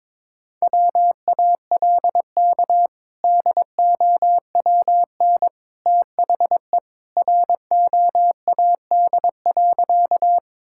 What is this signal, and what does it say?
Morse code, heard as a keyed tone, about 22 wpm, 700 Hz. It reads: WALK DOWN THE ROAD.
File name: practice / 22wpm / Sentences with 2-Word Phrases - 22wpm